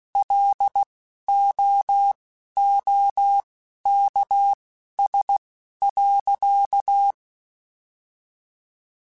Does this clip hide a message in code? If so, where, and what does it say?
Yes: Morse code from 0.2 to 7.1 s, heard as a keyed tone, about 16 wpm, spaced as for 11 wpm, 800 Hz. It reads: LOOKS.